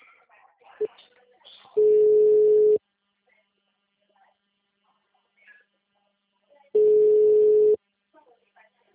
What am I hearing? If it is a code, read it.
Ringback tone (Europe, 425 Hz), with speech.
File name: out- -2020- - - .123